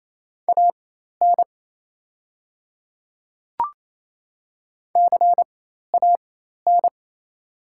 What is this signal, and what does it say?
Morse code, heard as a keyed tone, about 28 wpm, spaced as for 13 wpm, 690 Hz. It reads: AN CAN
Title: 4x Character Spacing - Top 100 Words - 28wpm